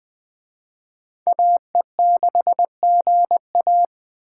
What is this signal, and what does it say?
Morse code, heard as a keyed tone, about 20 wpm, 700 Hz. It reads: AE6GA
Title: Speed Racing - USA 2x2 Call Signs 17wpm